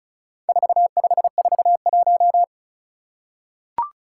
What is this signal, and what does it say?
Morse code, heard as a keyed tone, about 36 wpm, 700 Hz. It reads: 4541
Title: Four Challenging Letter-Number Combinations 35wpm